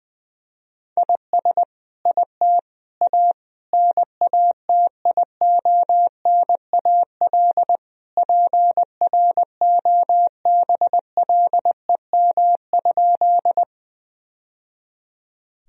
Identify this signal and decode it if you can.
Morse code, heard as a keyed tone, about 20 wpm, 700 Hz. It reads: IS IT A NATIONAL PROBLEM?
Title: Sentences from Top 1000 Words - Review Part 4 - 20wpm